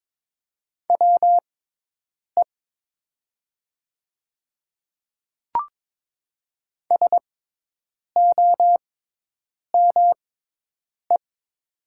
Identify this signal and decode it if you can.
Morse code, heard as a keyed tone, about 22 wpm, spaced as for 8 wpm, 700 Hz. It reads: WE SOME
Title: 6x Character Spacing - Top 300 Words - 22wpm